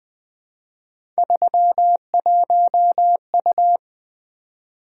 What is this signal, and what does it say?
Morse code, heard as a keyed tone, about 20 wpm, 700 Hz. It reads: 31U
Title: Three Letter-Number Combinations - 20wpm